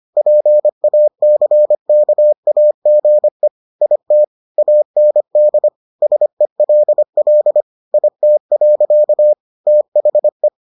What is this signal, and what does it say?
Morse code, heard as a keyed tone, about 25 wpm, 600 Hz. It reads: PACKAGE IT AND SELL IT. THE